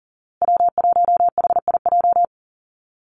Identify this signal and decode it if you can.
Morse code, heard as a keyed tone, about 37 wpm, 700 Hz. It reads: W1HIJ
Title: Call Signs Repeated 2x - 40wpm